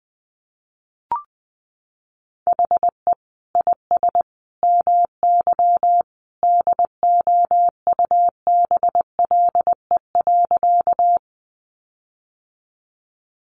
Morse code, heard as a keyed tone, about 20 wpm, 700 Hz. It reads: HE IS MY DOUBLE.
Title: Sentences from Top 1400 Words 20wpm